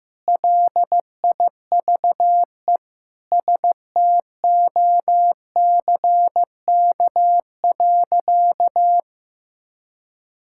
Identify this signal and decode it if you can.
Morse code, heard as a keyed tone, about 15 wpm, 700 Hz. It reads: LIVE STOCK.
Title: Sentences from Top 1000 Words - Review Part 5 - 15wpm